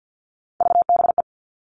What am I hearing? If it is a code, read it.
Morse code, heard as a keyed tone, about 50 wpm, 700 Hz. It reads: VBE
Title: All Cut Numbers - 3-Digit 50wpm